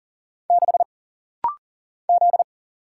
Morse code, heard as a keyed tone, about 40 wpm, 700 Hz. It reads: 6 Z
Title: Single Letter-Number Alt - Rapid-Fire 40wpm